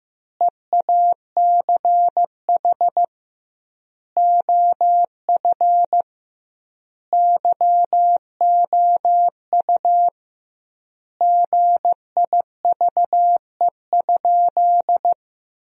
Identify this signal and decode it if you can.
Morse code, heard as a keyed tone, about 15 wpm, 700 Hz. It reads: EACH OF YOU GIVE?
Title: 2x Word-Spacing - Sentences from Top 200 Words - 15wpm